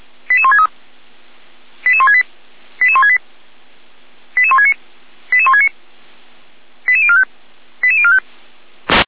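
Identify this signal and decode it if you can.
Five-tone selective call (ZVEI), heard as quick runs of five tones, tones of about 70 ms. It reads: 88163 88168 88168 88169 88169 88045 88045
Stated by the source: ZVEI five-tone call received on SDR, fire brigade Orlamünde (feuerwehr-orlamuende) recording.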